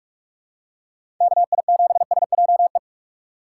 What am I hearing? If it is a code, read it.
Morse code, heard as a keyed tone, about 45 wpm, 700 Hz. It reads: KI7SJE
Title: USA Call Signs - 6 Characters - 4x Character Spacing - 45wpm